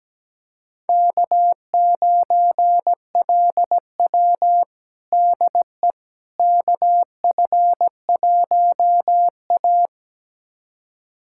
Morse code, heard as a keyed tone, about 17 wpm, 700 Hz. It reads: K9LW DE KF1A
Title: USA Callsign DE Callsign - 4 Characters - 17wpm